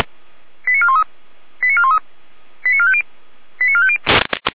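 Five-tone selective call (ZVEI), heard as quick runs of five tones, tones of about 70 ms. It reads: 89413 89413 89450 89450
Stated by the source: Five-tone ZVEI selective call 88379, received by SDR.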